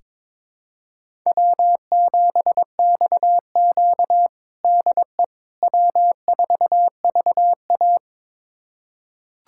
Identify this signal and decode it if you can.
Morse code, heard as a keyed tone, about 22 wpm, 700 Hz. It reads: W7XQ DE W4VA
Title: USA Callsign DE Callsign - 4 Characters - 22wpm